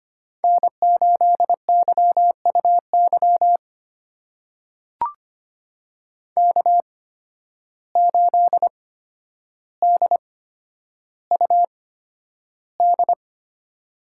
Morse code, heard as a keyed tone, about 25 wpm, spaced as for 11 wpm, 700 Hz. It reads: N8YUY K8DUD
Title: USA Call Signs - 5 Characters - 8x Character Spacing - 25wpm